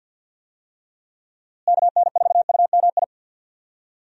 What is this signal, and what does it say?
Morse code, heard as a keyed tone, about 46 wpm, 700 Hz. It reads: KN4UMI
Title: USA Call Signs - 6 Characters - Repeated 3x - 50wpm